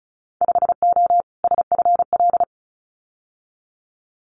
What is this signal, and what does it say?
Morse code, heard as a keyed tone, about 34 wpm, 700 Hz. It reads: 5O SFL
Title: Winter Field Day Exchanges 35wpm